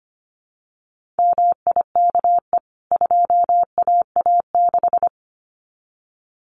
Morse code, heard as a keyed tone, about 25 wpm, 700 Hz. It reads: MIKE 2AA6
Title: CWT Exchange 25wpm